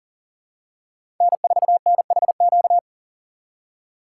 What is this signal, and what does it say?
Morse code, heard as a keyed tone, about 40 wpm, 700 Hz. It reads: N4NHQ
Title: USA Call Signs - 5 Characters - 3x Character Spacing - 40wpm